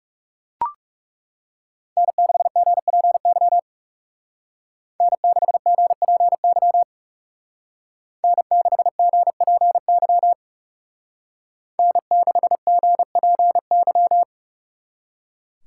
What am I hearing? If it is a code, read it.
Morse code, heard as a keyed tone, about 39 wpm, 700 Hz. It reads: N6GPY N6GPY N6GPY N6GPY